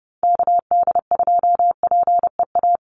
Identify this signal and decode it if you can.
Morse code, heard as a keyed tone, about 30 wpm, 700 Hz. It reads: KD2PEA
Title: USA Call Signs 30wpm